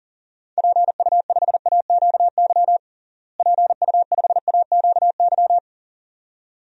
Morse code, heard as a keyed tone, about 40 wpm, 700 Hz. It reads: PU5AQY PU5AQY
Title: DXCC-100 Call Signs - 6 Characters - Repeated 2x - 40wpm